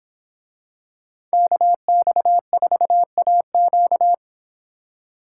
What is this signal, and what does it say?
Morse code, heard as a keyed tone, about 26 wpm, 700 Hz. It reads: KX4AQ